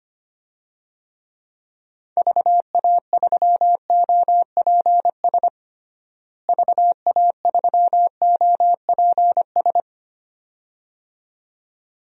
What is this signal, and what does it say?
Morse code, heard as a keyed tone, about 25 wpm, 700 Hz. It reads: VA3OPS VA3OPS
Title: DXCC-100 Call Signs - 6 Characters - Repeated 2x - 25wpm